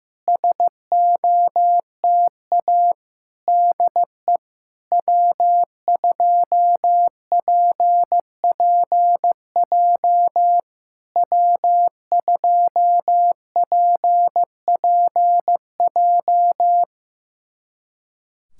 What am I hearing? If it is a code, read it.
Morse code, heard as a keyed tone, about 15 wpm, 700 Hz. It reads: SOTA DE W2PPJ W2PPJ
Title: SOTA QSO Practice 15wpm